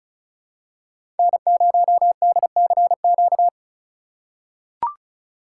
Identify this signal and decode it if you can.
Morse code, heard as a keyed tone, about 36 wpm, 700 Hz. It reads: N0DCQ
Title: USA Call Signs - 5 Characters - 2x Character Spacing - 35wpm